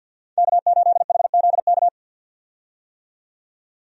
Morse code, heard as a keyed tone, about 47 wpm, 700 Hz. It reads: K8HZK
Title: USA Call Signs - 5 Characters - 50wpm